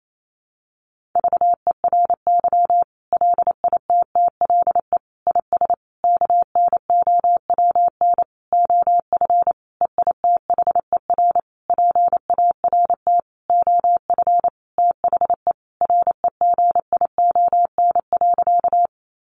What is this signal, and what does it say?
Morse code, heard as a keyed tone, about 28 wpm, 700 Hz. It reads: VERY LITTLE IS KNOWN OF EITHER PART OF THE REGION.
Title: Sentences from Top 1500 Words 28wpm